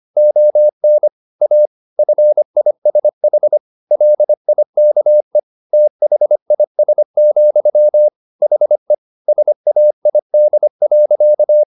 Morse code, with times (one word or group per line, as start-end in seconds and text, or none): 0.2-1.1 ON
1.4-1.7 A
2.0-3.6 FISH
3.9-5.4 LIKE
5.7-8.1 THIS,
8.4-8.9 HE
9.3-11.6 SAID.